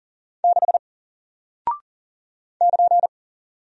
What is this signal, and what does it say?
Morse code, heard as a keyed tone, about 40 wpm, 700 Hz. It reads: 6 <KN>
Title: Single Character (Letter:Number:Punctuation:Prosign) Alt 40wpm